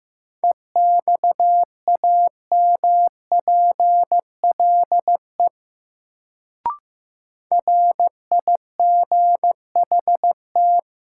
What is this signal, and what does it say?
Morse code, heard as a keyed tone, about 16 wpm, 700 Hz. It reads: EXAMPLE RIGHT
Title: Sets of 2 Words - New Word Drill - 15wpm